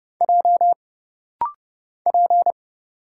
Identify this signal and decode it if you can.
Morse code, heard as a keyed tone, about 30 wpm, 700 Hz. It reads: J P
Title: Single Letter-Number Alt - Rapid-Fire 30wpm